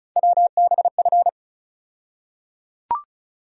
Morse code, heard as a keyed tone, about 36 wpm, 700 Hz. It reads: WBF
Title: USA Call Sign Postfixes - 3 Letters 35wpm